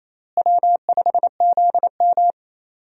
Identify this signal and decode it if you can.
Morse code, heard as a keyed tone, about 28 wpm, 700 Hz. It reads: W5ZM
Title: USA Call Signs - 4 Characters - Repeated 2x - 28wpm